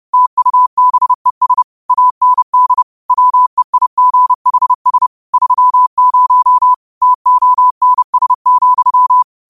Morse code, heard as a keyed tone, about 30 wpm, 1000 Hz. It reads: TUBES AND WEIGHS 30 TONS,
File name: qotdfiles / 30wpm